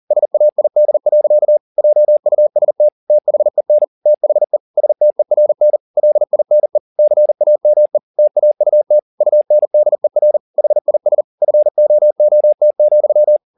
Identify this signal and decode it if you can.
Morse code, heard as a keyed tone, about 40 wpm, 600 Hz. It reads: SAID. JUST THEN THE STERN LINE CAME TAUT UNDER HIS FOOT,